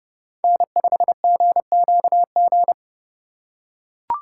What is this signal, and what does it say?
Morse code, heard as a keyed tone, about 30 wpm, 700 Hz. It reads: N5GQG